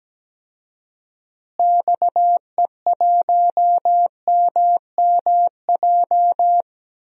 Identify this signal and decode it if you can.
Morse code, heard as a keyed tone, about 17 wpm, 700 Hz. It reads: XE1MMJ
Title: DXCC-100 Call Signs - 6 Characters - Repeated 2x - 17wpm